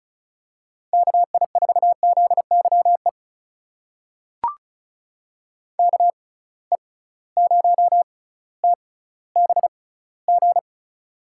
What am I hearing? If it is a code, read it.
Morse code, heard as a keyed tone, about 34 wpm, spaced as for 18 wpm, 700 Hz. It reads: KI4ZYE KE0TBG
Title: USA Call Signs - 6 Characters - 6x Character Spacing - 35wpm